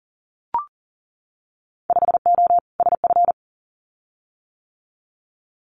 Morse code, heard as a keyed tone, about 37 wpm, 700 Hz. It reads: 5O SF
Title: Winter Field Day Exchanges Encore 40wpm